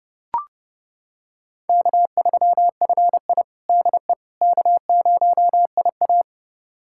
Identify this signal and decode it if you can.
Morse code, heard as a keyed tone, about 30 wpm, 700 Hz. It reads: K3FI DE K0IA
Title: USA Callsign DE Callsign - 4 Characters - 30wpm